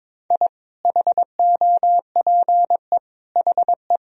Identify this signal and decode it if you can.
Morse code, heard as a keyed tone, about 22 wpm, 700 Hz. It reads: I HOPE HE
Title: Sentences from Top 1000 Words - Review Part 1 - 22wpm